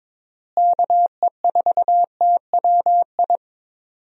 Morse code, heard as a keyed tone, about 22 wpm, 700 Hz. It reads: KE4TWI
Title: USA Call Signs - 6 Characters - Repeated 3x - 22wpm